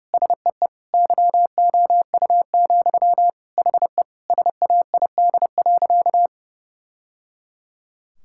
Morse code, heard as a keyed tone, about 32 wpm, 700 Hz. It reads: SEE YOU, HE SAID.
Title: Sentences from Top 1000 Words - Review Part 6 - 30wpm